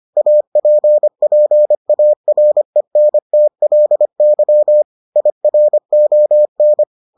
Morse code, heard as a keyed tone, about 25 wpm, 600 Hz. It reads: APPARENTLY IRON